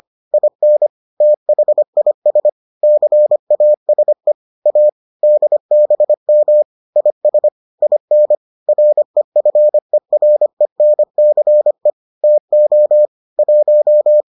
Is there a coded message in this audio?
Yes: Morse code from 0.3 to 14.2 s, heard as a keyed tone, about 25 wpm, 600 Hz. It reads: IN THIS CASE A DBM IS IN REFERENCE TO 1